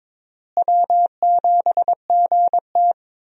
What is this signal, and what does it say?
Morse code, heard as a keyed tone, about 21 wpm, 700 Hz. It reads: W7GT
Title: USA Call Signs - 4 Characters - Repeated 3x - 22wpm